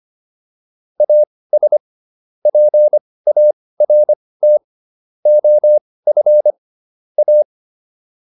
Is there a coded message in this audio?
Yes: Morse code from 1.0 to 7.4 s, heard as a keyed tone, about 25 wpm, spaced as for 18 wpm, 600 Hz. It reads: AS PART OF A